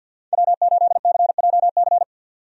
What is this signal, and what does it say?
Morse code, heard as a keyed tone, about 49 wpm, 700 Hz. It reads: W8CJC
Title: USA Call Signs - 5 Characters - 50wpm